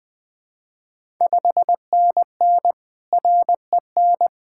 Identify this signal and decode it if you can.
Morse code, heard as a keyed tone, about 20 wpm, 700 Hz. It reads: 5NN REN
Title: Ontario QSO Party Exchange 20wpm